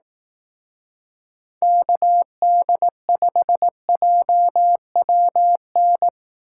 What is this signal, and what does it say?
Morse code, heard as a keyed tone, about 18 wpm, 700 Hz. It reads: KD5JWN
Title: Speed Racing - USA Call Signs 15wpm